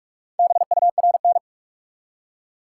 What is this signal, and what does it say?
Morse code, heard as a keyed tone, about 44 wpm, 700 Hz. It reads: BURN
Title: Four Letter Words - 45wpm